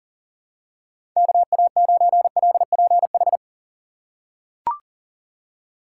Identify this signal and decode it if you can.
Morse code, heard as a keyed tone, about 40 wpm, 700 Hz. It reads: KA9LPH
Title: USA Call Signs - 4x Character Spacing - 40wpm